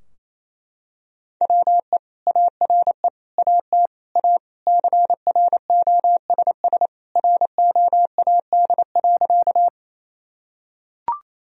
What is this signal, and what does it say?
Morse code, heard as a keyed tone, about 27 wpm, 700 Hz. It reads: WE ARE AT A CROSS ROAD.
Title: Sentences from Top 1000 Words - Review Part 2 - 28wpm